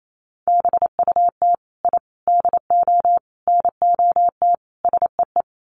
Morse code, heard as a keyed tone, about 28 wpm, 700 Hz. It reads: BUT I DO NOT SEE
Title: Sets of 5 Words Encore 28wpm